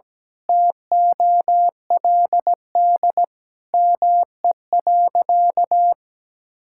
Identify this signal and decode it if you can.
Morse code, heard as a keyed tone, about 17 wpm, 700 Hz. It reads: TOLD ME.
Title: Sentences from Top 1000 Words - Review Part 6 - 17wpm